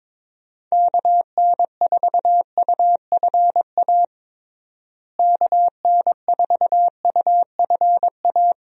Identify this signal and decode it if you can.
Morse code, heard as a keyed tone, about 22 wpm, 700 Hz. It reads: KN4UFA KN4UFA
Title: USA Call Signs - 6 Characters - Repeated 3x - 22wpm